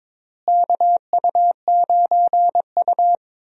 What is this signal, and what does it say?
Morse code, heard as a keyed tone, about 22 wpm, 700 Hz. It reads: KU9U
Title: USA Call Signs - 4 Characters - Repeated 3x - 22wpm